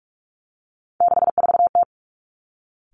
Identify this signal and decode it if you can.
Morse code, heard as a keyed tone, about 46 wpm, 700 Hz. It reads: 64T
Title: Three Challenging Letter-Number Combinations 45wpm